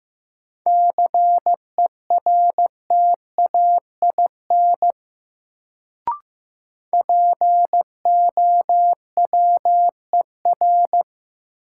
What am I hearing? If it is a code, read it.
Morse code, heard as a keyed tone, about 15 wpm, 700 Hz. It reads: CERTAIN POWER
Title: Sentences from Top 500 Words - New Word Drill - 15wpm